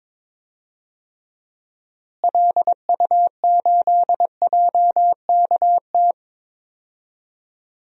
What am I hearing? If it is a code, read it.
Morse code, heard as a keyed tone, about 22 wpm, 700 Hz. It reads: LU8JKT
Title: DXCC-100 Call Signs - 6 Characters - Repeated 3x - 22wpm